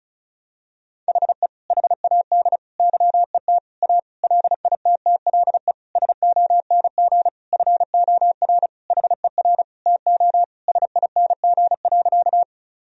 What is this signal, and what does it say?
Morse code, heard as a keyed tone, about 34 wpm, 700 Hz. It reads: HE HAD YET A LITTLE SONG FOR HER TO SING.
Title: Sentences from Top 1000 Words - Review Part 2 - 35wpm